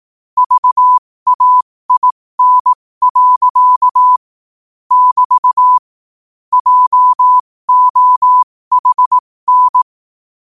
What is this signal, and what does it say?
Morse code, heard as a keyed tone, about 18 wpm, spaced as for 15 wpm, 1000 Hz. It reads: VAIN. <BT> JOHN
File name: qotdfiles / 15wpm